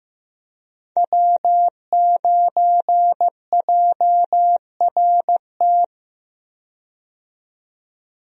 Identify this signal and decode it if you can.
Morse code, heard as a keyed tone, about 15 wpm, 700 Hz. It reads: W9JRT
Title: USA Call Signs - 5 Characters - 15wpm